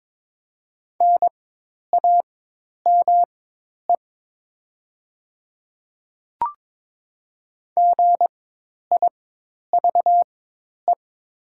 Morse code, heard as a keyed tone, about 22 wpm, spaced as for 10 wpm, 700 Hz. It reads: NAME GIVE
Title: 4x Character Spacing - Top 200 Words - 22wpm